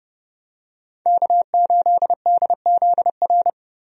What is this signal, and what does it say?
Morse code, heard as a keyed tone, about 30 wpm, 700 Hz. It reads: K8DZR
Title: USA Call Signs - 5 Characters - Repeated 2x - 30wpm